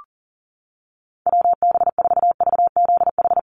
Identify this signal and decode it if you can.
Morse code, heard as a keyed tone, about 40 wpm, 700 Hz. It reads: WB4VZH
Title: USA Call Signs 40wpm